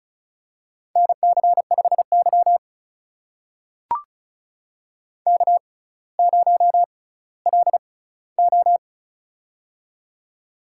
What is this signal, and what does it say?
Morse code, heard as a keyed tone, about 36 wpm, spaced as for 18 wpm, 700 Hz. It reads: NC5Y K0LO
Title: USA Call Signs - 4 Characters - 6x Character Spacing - 35wpm